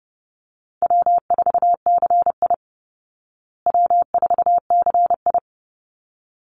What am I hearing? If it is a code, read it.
Morse code, heard as a keyed tone, about 30 wpm, 700 Hz. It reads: W4CI W4CI
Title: Call Signs Repeated 3x - 30wpm